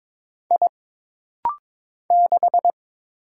Morse code, heard as a keyed tone, about 22 wpm, 700 Hz. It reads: I 6
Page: This rapid-fire timing is geared to push your ICR (Instant Character Recognition) skills.